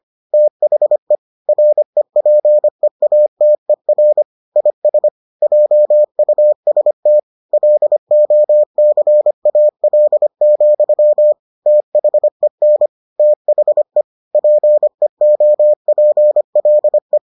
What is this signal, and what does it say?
Morse code, heard as a keyed tone, about 25 wpm, 600 Hz. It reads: THE REPEATER IS JUST LOCAL, THEN THE PEOPLE